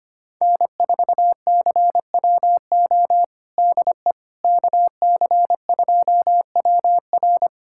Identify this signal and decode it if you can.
Morse code, heard as a keyed tone, about 25 wpm, 700 Hz. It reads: N4CWO DE KC2WR
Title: USA Callsign DE Callsign - 5 Characters - 25wpm